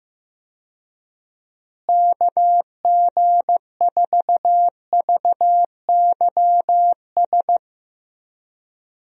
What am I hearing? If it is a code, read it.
Morse code, heard as a keyed tone, about 15 wpm, 700 Hz. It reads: KG4VYS
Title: USA Call Signs - 6 Characters - Repeated 2x - 15wpm